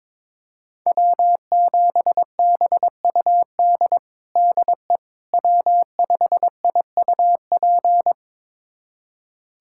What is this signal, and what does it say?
Morse code, heard as a keyed tone, about 22 wpm, 700 Hz. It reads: W7BUD DE W5IUP
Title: USA Callsign DE Callsign - 5 Characters - 22wpm